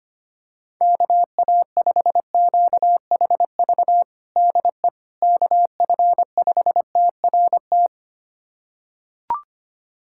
Morse code, heard as a keyed tone, about 25 wpm, 700 Hz. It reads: KA5QHV DE KF5TRT
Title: USA Callsign DE Callsign - 6 Characters - 25wpm